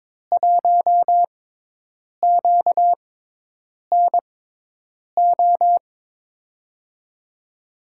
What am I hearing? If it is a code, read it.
Morse code, heard as a keyed tone, about 22 wpm, spaced as for 8 wpm, 700 Hz. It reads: 1QNO
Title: USA Call Signs - 5 Characters - 6x Character Spacing - 22wpm